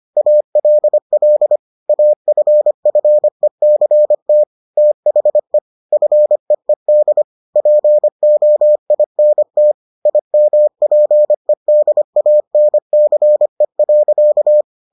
Morse code, heard as a keyed tone, about 25 wpm, 600 Hz. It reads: ALL AFFECT THE FEED POINT IMPEDANCE.